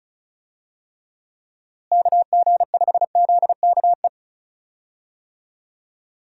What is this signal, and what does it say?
Morse code, heard as a keyed tone, about 36 wpm, 700 Hz. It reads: KG5ZKE
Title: USA Call Signs - 6 Characters - Repeated 2x - 35wpm